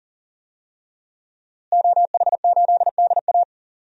Morse code, heard as a keyed tone, about 40 wpm, 700 Hz. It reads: OH8DA
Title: DXCC-100 Call Signs - 5 Characters - Repeated 2x - 40wpm